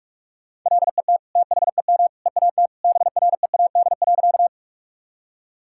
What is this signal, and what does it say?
Morse code, heard as a keyed tone, about 41 wpm, 700 Hz. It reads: LET THEM EAT BREAD.
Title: Sentences from Top 1600 Words 45wpm